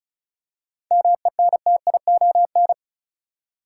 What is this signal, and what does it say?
Morse code, heard as a keyed tone, about 36 wpm, 700 Hz. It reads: MENTION
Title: Seven Letter Words - 35wpm